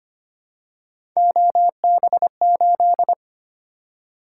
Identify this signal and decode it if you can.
Morse code, heard as a keyed tone, about 25 wpm, 700 Hz. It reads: OB8
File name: Three Letter-Number Combinations - 25wpm